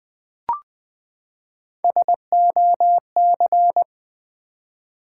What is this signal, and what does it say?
Morse code, heard as a keyed tone, about 20 wpm, 700 Hz. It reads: SOC
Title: Build A Word - 3 Letters At A Time - 9-Letter Words - 20wpm